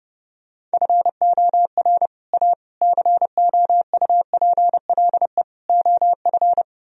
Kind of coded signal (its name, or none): Morse code